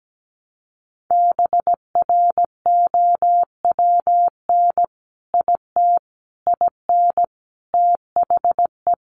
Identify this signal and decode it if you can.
Morse code, heard as a keyed tone, about 17 wpm, 700 Hz. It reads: BROWN IT IN THE
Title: Sentences from Top 1200 Words 17wpm